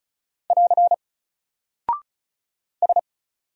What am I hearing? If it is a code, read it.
Morse code, heard as a keyed tone, about 36 wpm, 700 Hz. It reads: <AR> S
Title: Single Character (Letter:Number:Punctuation:Prosign) 35wpm